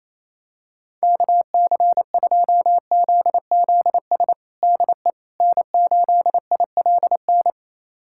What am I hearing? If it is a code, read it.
Morse code, heard as a keyed tone, about 28 wpm, 700 Hz. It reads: KC2ZZS DE N8ILN
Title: USA Callsign DE Callsign 28wpm